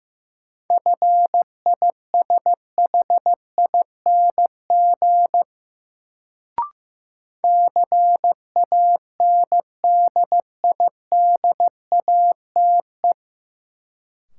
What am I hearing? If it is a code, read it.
Morse code, heard as a keyed tone, about 15 wpm, 700 Hz. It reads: FISHING CANDIDATE
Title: Sentences from Top 1600 Words - New Word Drill Alt - 15wpm